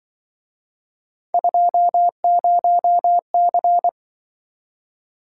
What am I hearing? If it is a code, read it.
Morse code, heard as a keyed tone, about 25 wpm, 700 Hz. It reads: 20C